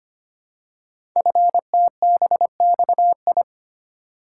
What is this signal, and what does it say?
Morse code, heard as a keyed tone, about 25 wpm, 700 Hz. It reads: FTBXI
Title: Code Groups - 5 Letters - 25wpm